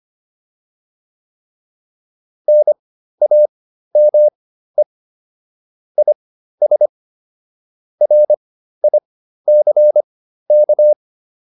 Morse code, heard as a keyed tone, about 25 wpm, spaced as for 13 wpm, 600 Hz. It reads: NAME IS RICK